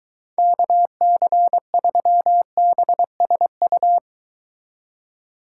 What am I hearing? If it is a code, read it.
Morse code, heard as a keyed tone, about 23 wpm, 700 Hz. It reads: KC3BSU